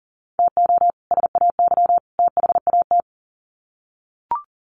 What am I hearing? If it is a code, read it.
Morse code, heard as a keyed tone, about 40 wpm, 700 Hz. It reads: TO SAY THAT
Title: Sets of 3 Words 40wpm